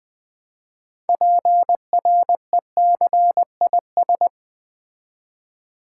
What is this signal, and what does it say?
Morse code, heard as a keyed tone, about 20 wpm, 700 Hz. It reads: PRECIS
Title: Build A Word - 3 Letters At A Time - 9-Letter Words - 20wpm